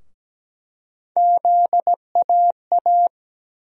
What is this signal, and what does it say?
Morse code, heard as a keyed tone, about 17 wpm, 700 Hz. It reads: ZAA
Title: USA Call Sign Postfixes - 3 Letters 17wpm